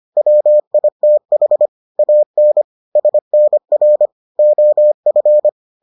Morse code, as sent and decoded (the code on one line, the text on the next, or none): .-- .. - .... / .- -. / ... -. .-. / --- ..-.
WITH AN SNR OF